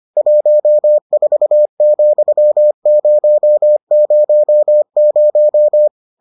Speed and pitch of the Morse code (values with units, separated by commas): 25 wpm, 600 Hz